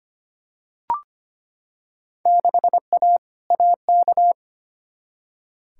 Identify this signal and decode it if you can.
Morse code, heard as a keyed tone, about 25 wpm, 700 Hz. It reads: 6A AK
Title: Field Day Exchanges Encore - 25wpm